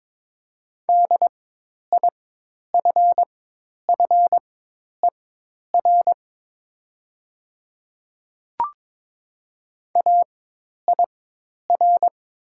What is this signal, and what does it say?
Morse code, heard as a keyed tone, about 22 wpm, spaced as for 10 wpm, 700 Hz. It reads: DIFFER AIR
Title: 4x Character Spacing - Top 300 Words - 22wpm